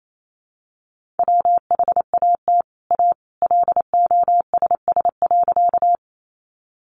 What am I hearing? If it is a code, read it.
Morse code, heard as a keyed tone, about 28 wpm, 700 Hz. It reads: WHAT A LOSS.
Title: Sentences from Top 1400 Words 28wpm